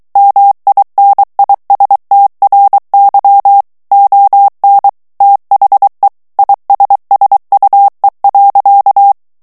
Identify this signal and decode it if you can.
Morse code, heard as a keyed tone, about 23 wpm, 800 Hz. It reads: MINISTRY ON THE ISSUE.
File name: morse mp3 / ANGLAIS texte clair 2 vitesse 22 mots minute